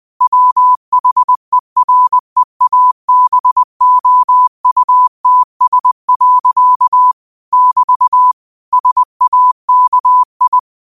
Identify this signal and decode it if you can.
Morse code, heard as a keyed tone, about 20 wpm, 1000 Hz. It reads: WHEREABOUTS. <BT> SAKI
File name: qotdfiles / 20wpm